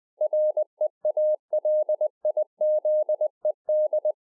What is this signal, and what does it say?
Morse code, heard as a keyed tone, about 20 wpm, 610 Hz. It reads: REALIZED